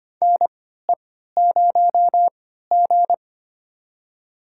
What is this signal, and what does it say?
Morse code, heard as a keyed tone, about 25 wpm, spaced as for 14 wpm, 700 Hz. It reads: NE0G
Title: USA Call Signs - 4 Characters - 3x Character Spacing - 25wpm